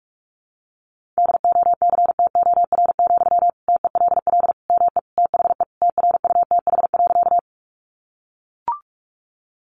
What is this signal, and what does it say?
Morse code, heard as a keyed tone, about 44 wpm, 700 Hz. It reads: DOCTOR, TELL ME THE TRUTH.
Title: Sentences from Top 1500 Words 45wpm